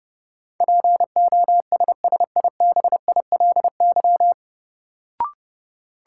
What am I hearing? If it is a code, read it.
Morse code, heard as a keyed tone, about 30 wpm, 700 Hz. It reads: POSSIBILY
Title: Sentences from Top 1300 Words - New Word Drill Alt - 30wpm